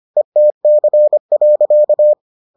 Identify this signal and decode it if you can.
Morse code, heard as a keyed tone, about 25 wpm, 600 Hz. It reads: ETC.